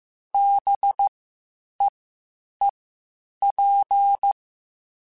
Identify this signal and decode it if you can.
Morse code, heard as a keyed tone, about 15 wpm, spaced as for 8 wpm, 800 Hz. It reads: BEEP